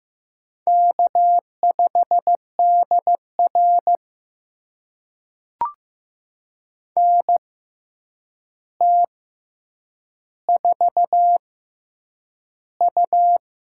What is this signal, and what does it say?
Morse code, heard as a keyed tone, about 15 wpm, spaced as for 8 wpm, 700 Hz. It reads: K5DR NT4U
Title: USA Call Signs - 4 Characters - 6x Character Spacing - 15wpm